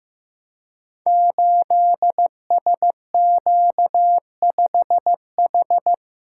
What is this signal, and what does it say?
Morse code, heard as a keyed tone, about 15 wpm, 700 Hz. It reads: 8SQ5H
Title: Code Groups - 5 Letter:Numbers - 15wpm